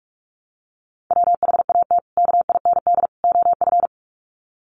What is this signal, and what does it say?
Morse code, heard as a keyed tone, about 44 wpm, 700 Hz. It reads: WHAT KIND OF
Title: Sets of 3 Words 45wpm